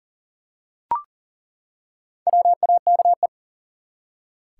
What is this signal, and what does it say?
Morse code, heard as a keyed tone, about 40 wpm, 700 Hz. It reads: WAKE
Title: Four Letter Words - 40wpm